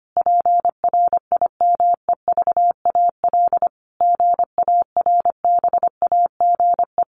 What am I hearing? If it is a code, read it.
Morse code, heard as a keyed tone, about 25 wpm, 700 Hz. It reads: PRIMEVAL GARBAGE
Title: A Sound Of Thunder 25wpm